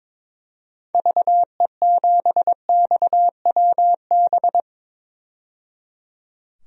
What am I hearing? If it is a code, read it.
Morse code, heard as a keyed tone, about 22 wpm, 700 Hz. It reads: VE7XWB
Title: DXCC-100 Call Signs - 6 Characters - 22wpm